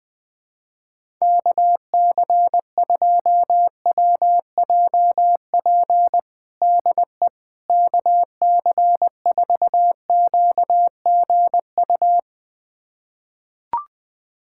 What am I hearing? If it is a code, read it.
Morse code, heard as a keyed tone, about 20 wpm, 700 Hz. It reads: KC2WJP DE KC4QGU
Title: USA Callsign DE Callsign - 6 Characters - 20wpm